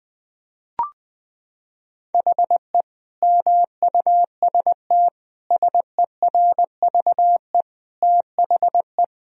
Morse code, heard as a keyed tone, about 20 wpm, 700 Hz. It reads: HE MUST SERVE THE
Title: Sentences from Top 1000 Words - Review Part 2 - 20wpm